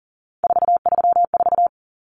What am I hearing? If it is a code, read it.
Morse code, heard as a keyed tone, about 40 wpm, 700 Hz. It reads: 434
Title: Common Cut Numbers - 3-Digit 40wpm